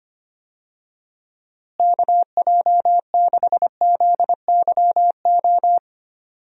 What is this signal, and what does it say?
Morse code, heard as a keyed tone, about 25 wpm, 700 Hz. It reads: KJ6ZYO
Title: USA Call Signs - 6 Characters - Repeated 3x - 25wpm